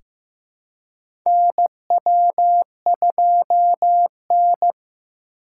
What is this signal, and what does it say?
Morse code, heard as a keyed tone, about 15 wpm, 700 Hz. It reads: NW2N
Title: USA Call Signs - 2x Character Spacing - 15wpm